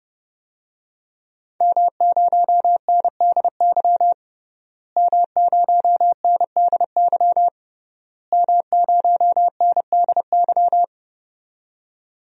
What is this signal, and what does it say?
Morse code, heard as a keyed tone, about 30 wpm, 700 Hz. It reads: M0NDY M0NDY M0NDY
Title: DXCC-100 Call Signs - 5 Characters - Repeated 3x - 30wpm